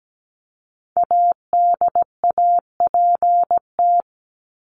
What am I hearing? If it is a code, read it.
Morse code, heard as a keyed tone, about 17 wpm, 700 Hz. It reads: ADAPT
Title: Top 1501-1750 Words 17wpm